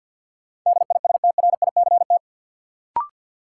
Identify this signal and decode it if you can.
Morse code, heard as a keyed tone, about 51 wpm, 700 Hz. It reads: DISTRICT
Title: Sentences from Top 1600 Words - New Word Drill Alt - 50wpm